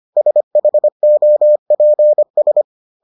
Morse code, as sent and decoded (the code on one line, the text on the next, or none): ... .... --- .--. ...
SHOPS